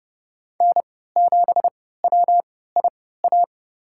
Morse code, heard as a keyed tone, about 31 wpm, spaced as for 17 wpm, 700 Hz. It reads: N7WIA
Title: USA Call Signs - 3x Character Spacing - 30wpm